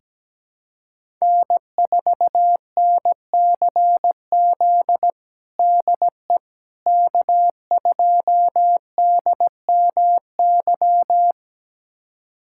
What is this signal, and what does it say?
Morse code, heard as a keyed tone, about 17 wpm, 700 Hz. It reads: N4NCZ DE K2DMY
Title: USA Callsign DE Callsign - 5 Characters - 17wpm